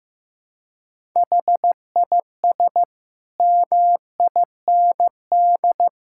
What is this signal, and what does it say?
Morse code, heard as a keyed tone, about 16 wpm, 700 Hz. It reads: HIS MIND
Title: Sentences from Top 1000 Words - Review Part 6 - 15wpm